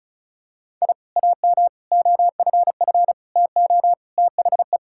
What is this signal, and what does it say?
Morse code, heard as a keyed tone, about 34 wpm, 700 Hz. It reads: I AM OFF TO THE
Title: Sentences from Top 1000 Words - Review Part 5 - 35wpm